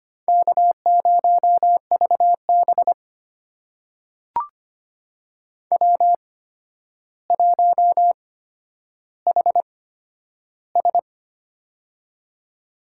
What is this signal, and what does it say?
Morse code, heard as a keyed tone, about 25 wpm, spaced as for 11 wpm, 700 Hz. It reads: K0VB W1HS
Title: USA Call Signs - 4 Characters - 8x Character Spacing - 25wpm